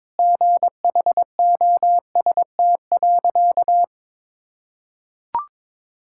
Morse code, heard as a keyed tone, about 22 wpm, 700 Hz. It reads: GHOST.
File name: practice / 22wpm / Sentences with 8-Word Phrases - 22wpm